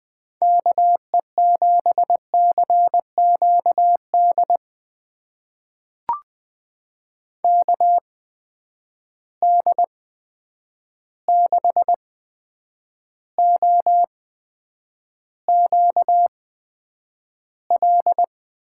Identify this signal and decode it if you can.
Morse code, heard as a keyed tone, about 21 wpm, spaced as for 9 wpm, 700 Hz. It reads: KE7CQD KD6OQL
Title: USA Call Signs - 6 Characters - 8x Character Spacing - 20wpm